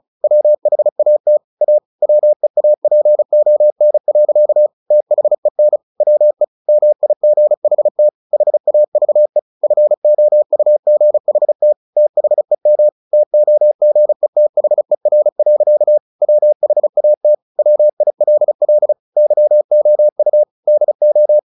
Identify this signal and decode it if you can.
Morse code, heard as a keyed tone, about 36 wpm, 600 Hz. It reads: WHAT A WEAPON. THEN WE MIGHT HAVE FOUGHT THEM TOGETHER. WHAT WILL YOU DO
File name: the old man and the sea 2 35wpm